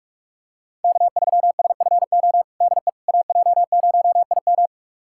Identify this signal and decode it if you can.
Morse code, heard as a keyed tone, about 41 wpm, 700 Hz. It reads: K3SFO DE AJ0IM
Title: USA Callsign DE Callsign - 5 Characters - 45wpm